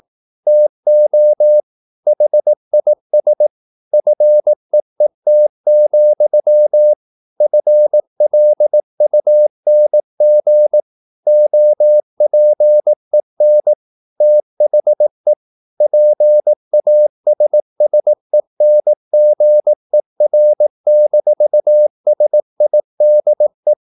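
Morse code, heard as a keyed tone, about 18 wpm, 600 Hz. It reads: TO HIS FEET, FLUNG OPEN THE PASSENGER-SIDE
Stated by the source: Morse practice at 18 wpm, English text, from CW med Gnister.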